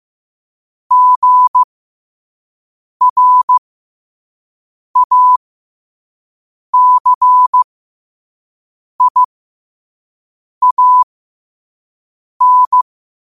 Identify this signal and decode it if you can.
Morse code, heard as a keyed tone, about 15 wpm, spaced as for 5 wpm, 1000 Hz. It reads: GRACIAN
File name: qotdfiles / 5wpm